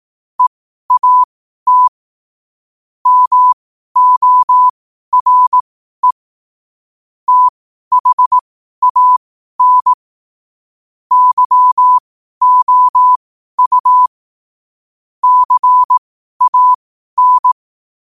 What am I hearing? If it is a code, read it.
Morse code, heard as a keyed tone, about 18 wpm, spaced as for 12 wpm, 1000 Hz. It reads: EAT MORE THAN YOU CAN
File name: qotdfiles / 12wpm